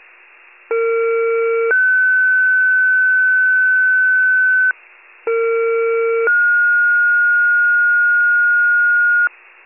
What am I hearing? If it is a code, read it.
Two-tone page, heard as a one-second tone, then a three-second tone, about 460 then 1500 Hz, about 460 then 1430 Hz. It is heard from 0.7 to 9.3 s.
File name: wcalert